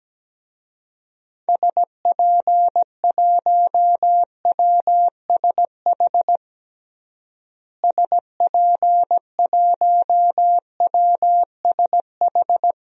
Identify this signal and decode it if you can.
Morse code, heard as a keyed tone, about 17 wpm, 700 Hz. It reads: SP1WSH SP1WSH
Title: DXCC-100 Call Signs - 6 Characters - Repeated 3x - 17wpm